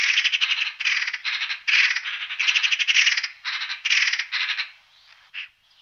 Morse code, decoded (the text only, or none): none